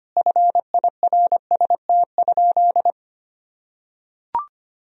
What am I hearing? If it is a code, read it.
Morse code, heard as a keyed tone, about 25 wpm, 700 Hz. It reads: FIRST?
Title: Sentences from Top 1000 Words - Review Part 3 - 25wpm